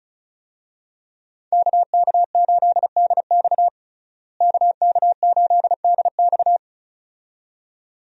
Morse code, heard as a keyed tone, about 34 wpm, 700 Hz. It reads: KK8DX KK8DX
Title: USA Call Signs - 5 Characters - Repeated 2x - 35wpm